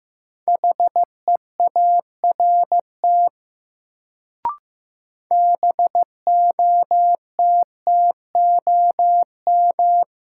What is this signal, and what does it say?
Morse code, heard as a keyed tone, about 15 wpm, 700 Hz. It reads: HEART BOTTOM
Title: Sets of 3 Words - New Word Drill - 15wpm